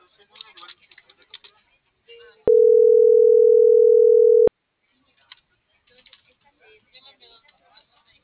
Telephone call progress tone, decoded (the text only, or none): ringback tone